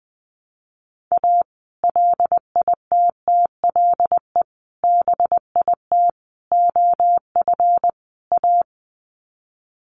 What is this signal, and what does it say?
Morse code, heard as a keyed tone, about 20 wpm, 700 Hz. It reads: A LITTLE BIT OF A
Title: Sets of 5 Words 20wpm